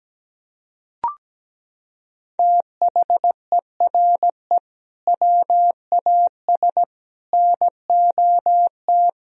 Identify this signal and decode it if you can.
Morse code, heard as a keyed tone, about 17 wpm, 700 Hz. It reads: THERE WAS NOT
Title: Sentences from Top 1000 Words - Review Part 2 - 17wpm